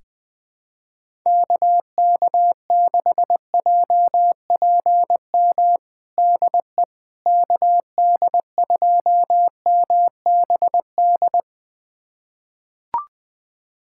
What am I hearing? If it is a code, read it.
Morse code, heard as a keyed tone, about 21 wpm, 700 Hz. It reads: KK6JPM DE KD2MBD
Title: USA Callsign DE Callsign - 6 Characters - 20wpm